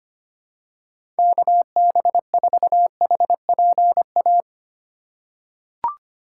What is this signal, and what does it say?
Morse code, heard as a keyed tone, about 25 wpm, 700 Hz. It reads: KB4HPA
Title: USA Call Signs - 6 Characters - 2x Character Spacing - 25wpm